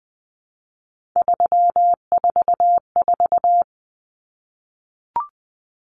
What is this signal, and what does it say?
Morse code, heard as a keyed tone, about 20 wpm, 700 Hz. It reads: 344
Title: Numbers - 3-Digit 20wpm